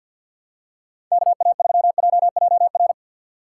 Morse code, heard as a keyed tone, about 53 wpm, 700 Hz. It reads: K A 3 J J R
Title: USA Call Signs - 6 Characters - 6x Character Spacing - 50wpm